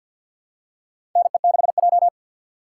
Morse code, heard as a keyed tone, about 49 wpm, 700 Hz. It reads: NE6J
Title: USA Call Signs - 4 Characters - 2x Character Spacing - 50wpm